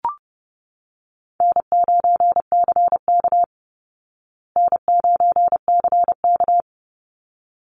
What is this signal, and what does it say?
Morse code, heard as a keyed tone, about 30 wpm, 700 Hz. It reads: N9CK N9CK
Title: Call Signs Repeated 3x - 30wpm